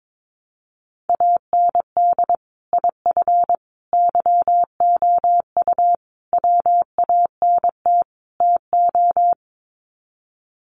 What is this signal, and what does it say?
Morse code, heard as a keyed tone, about 22 wpm, 700 Hz. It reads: AND IF YOU WANT TO
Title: Sets of 5 Words Encore 22wpm